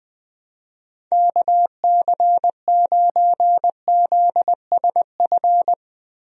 Morse code, heard as a keyed tone, about 20 wpm, 700 Hz. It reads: KC9ZSF
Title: USA Call Signs - 6 Characters - 20wpm